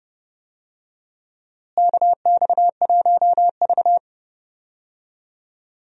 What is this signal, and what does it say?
Morse code, heard as a keyed tone, about 30 wpm, 700 Hz. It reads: KX1V
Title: USA Call Signs - 4 Characters - Repeated 3x - 30wpm